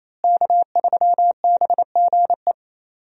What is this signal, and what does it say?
Morse code, heard as a keyed tone, about 28 wpm, 700 Hz. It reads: K3BGE
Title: USA Call Signs - 5 Characters - Repeated 2x - 28wpm